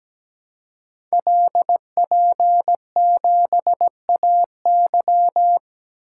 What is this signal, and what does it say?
Morse code, heard as a keyed tone, about 17 wpm, 700 Hz. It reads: LP7AY
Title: Code Groups - 5 Letter:Numbers - 17wpm